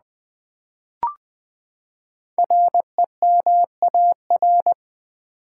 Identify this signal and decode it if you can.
Morse code, heard as a keyed tone, about 20 wpm, 700 Hz. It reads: REMAR
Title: Build A Word - 5 Letters At A Time - 10-Letter Words - 20wpm